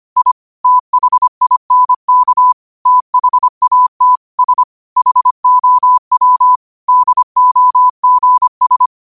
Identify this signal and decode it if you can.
Morse code, heard as a keyed tone, about 25 wpm, 1000 Hz. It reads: I THINK THAT S HOW DOGS
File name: qotdfiles / 25wpm